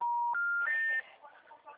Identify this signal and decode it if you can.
Special information tone, with noise.